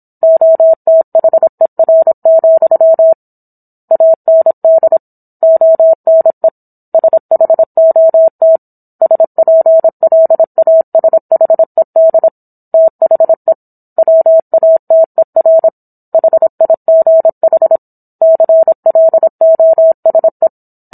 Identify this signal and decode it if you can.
Morse code, heard as a keyed tone, about 27 wpm, 650 Hz. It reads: OTHER, AND ONE SHOT SPLASHED THE WATER HIGH CLOSE